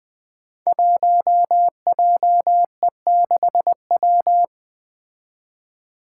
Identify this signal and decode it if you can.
Morse code, heard as a keyed tone, about 20 wpm, 700 Hz. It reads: 1JE6W
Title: Five Challenging Letter-Number Combinations 20wpm